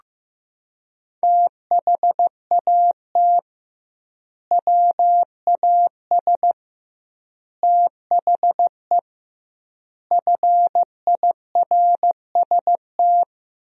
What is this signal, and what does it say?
Morse code, heard as a keyed tone, about 15 wpm, 700 Hz. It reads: THAT WAS THE FIRST
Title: 2x Word-Spacing - Sentences from Top 300 Words - 15wpm